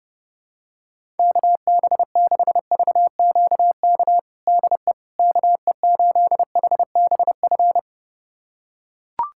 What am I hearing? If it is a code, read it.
Morse code, heard as a keyed tone, about 30 wpm, 700 Hz. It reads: KB6VQK DE KE8HBF
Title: USA Callsign DE Callsign - 6 Characters - 30wpm